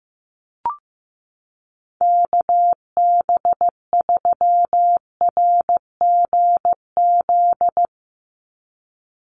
Morse code, heard as a keyed tone, about 15 wpm, 700 Hz. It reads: KB3RGZ
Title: USA Call Signs 15wpm